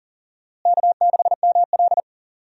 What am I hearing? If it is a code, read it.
Morse code, heard as a keyed tone, about 40 wpm, 700 Hz. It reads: K6ML
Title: USA Call Signs - 4 Characters - Repeated 2x - 40wpm